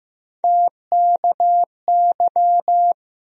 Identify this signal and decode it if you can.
Morse code, heard as a keyed tone, about 15 wpm, 700 Hz. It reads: TKY
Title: USA Call Sign Postfixes - 3 Letters 15wpm